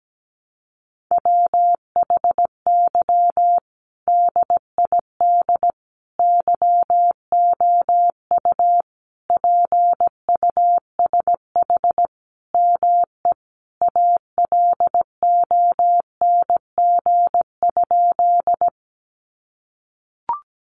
Morse code, heard as a keyed tone, about 17 wpm, 700 Hz. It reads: WHY DID YOU PUSH ME ALONG?
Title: Sentences from Top 1300 Words 17wpm